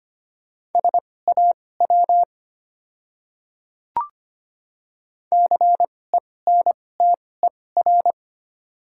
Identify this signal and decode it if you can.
Morse code, heard as a keyed tone, about 25 wpm, spaced as for 18 wpm, 700 Hz. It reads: SAW CENTER
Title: 2x Character Spacing - Top 500 Words - 25wpm